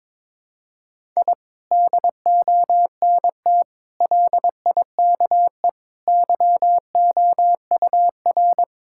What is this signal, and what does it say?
Morse code, heard as a keyed tone, about 21 wpm, 700 Hz. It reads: I DONT LIKE YOUR
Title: Sentences from Top 1000 Words - Review Part 6 - 22wpm